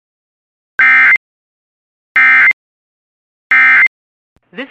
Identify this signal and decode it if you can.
Emergency Alert System (SAME) tones, heard as harsh buzzing bursts. It is heard from 0.8 to 3.9 s.